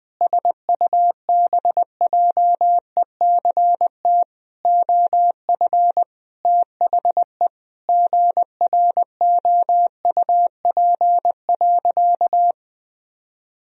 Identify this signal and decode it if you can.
Morse code, heard as a keyed tone, about 21 wpm, 700 Hz. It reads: SUBJECT OF THE GROUP.
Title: Sentences from Top 1000 Words - Review Part 1 - 20wpm